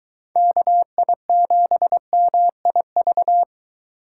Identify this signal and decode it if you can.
Morse code, heard as a keyed tone, about 23 wpm, 700 Hz. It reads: KI7MIV